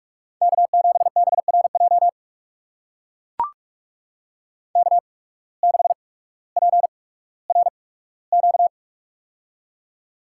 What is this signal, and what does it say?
Morse code, heard as a keyed tone, about 44 wpm, spaced as for 19 wpm, 700 Hz. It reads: K7BRJ K6PRQ
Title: USA Call Signs - 5 Characters - 8x Character Spacing - 45wpm